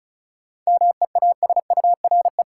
Morse code, heard as a keyed tone, about 36 wpm, 700 Hz. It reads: MEASURE
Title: Sentences from Top 1000 Words - Review Part 1 - New Word Drill - 35wpm